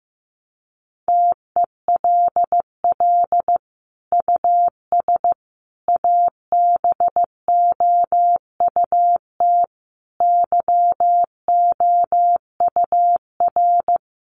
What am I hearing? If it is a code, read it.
Morse code, heard as a keyed tone, about 15 wpm, 700 Hz. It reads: TELL US ABOUT YOUR
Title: Sentences from Top 1200 Words 15wpm